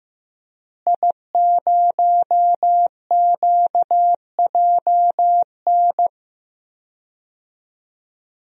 Morse code, heard as a keyed tone, about 15 wpm, 700 Hz. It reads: I0QJN
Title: DXCC-100 Call Signs - 5 Characters - Repeated 3x - 15wpm